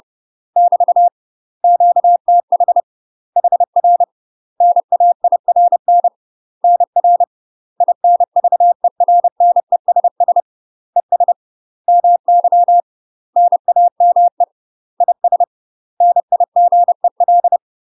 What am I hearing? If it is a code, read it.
Morse code, heard as a keyed tone, about 30 wpm, 700 Hz. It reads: <BT> QTH HR NAIRN NR INVERNESS ES MY NAME IS NIGEL
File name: UK MASTER QSOs TRAINING / 30WPM / UK MASTER QSOs 0040